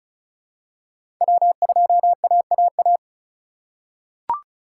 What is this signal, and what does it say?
Morse code, heard as a keyed tone, about 36 wpm, 700 Hz. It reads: W2AAA
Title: USA Call Signs - 5 Characters - 3x Character Spacing - 35wpm